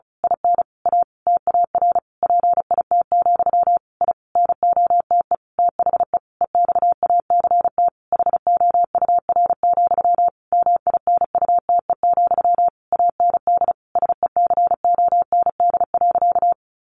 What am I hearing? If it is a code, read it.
Morse code, heard as a keyed tone, about 34 wpm, 700 Hz. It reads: IN A TAR PIT, I NOTE THE EXACT HOUR, MINUTE, AND SECOND.
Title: A Sound Of Thunder 35wpm